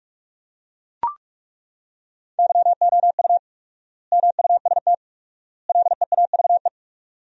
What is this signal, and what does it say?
Morse code, heard as a keyed tone, about 41 wpm, 700 Hz. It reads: YOU MUST LEAVE
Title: 4x Word-Spacing - Sentences from Top 400 Words - 45wpm